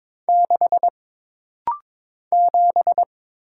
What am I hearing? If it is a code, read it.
Morse code, heard as a keyed tone, about 22 wpm, 700 Hz. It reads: 6 7
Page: This rapid-fire timing is geared to push your ICR (Instant Character Recognition) skills.